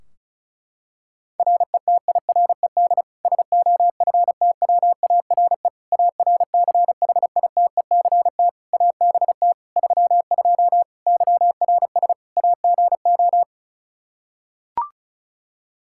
Morse code, heard as a keyed tone, about 34 wpm, 700 Hz. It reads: RETIRED SOFTWARE ARCHITECT ABT 32 YRS AGO
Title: Practice QSO - Exchange 3 - More Info - 35wpm